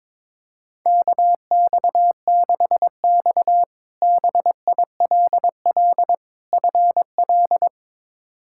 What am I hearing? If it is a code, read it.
Morse code, heard as a keyed tone, about 22 wpm, 700 Hz. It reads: KX6X BILL FL
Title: K1USN SST QSO Practice - 22wpm